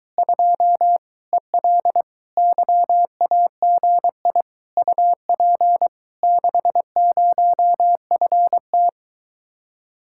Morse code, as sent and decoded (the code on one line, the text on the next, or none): ..--- / . .-.. / -.-- .- --. .. / ..- .--. / -.... ----- ..-. -
2 EL YAGI UP 60FT